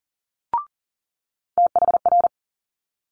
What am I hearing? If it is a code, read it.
Morse code, heard as a keyed tone, about 38 wpm, 700 Hz. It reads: THR
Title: Multi-letter Phonemes Quick Fire 40wpm